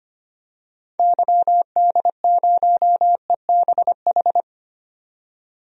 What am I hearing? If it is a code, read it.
Morse code, heard as a keyed tone, about 25 wpm, 700 Hz. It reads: YD0EBH
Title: DXCC-100 Call Signs - 6 Characters - 25wpm